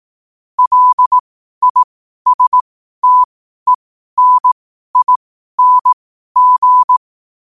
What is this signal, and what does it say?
Morse code, heard as a keyed tone, about 18 wpm, spaced as for 12 wpm, 1000 Hz. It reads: LISTENING
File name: qotdfiles / 12wpm